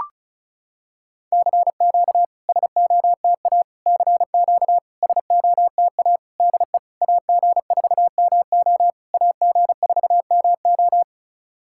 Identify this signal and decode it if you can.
Morse code, heard as a keyed tone, about 34 wpm, 700 Hz. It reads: CQ SOTA CQ SOTA DE AG4MO AG4MO
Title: SOTA QSO Practice 35wpm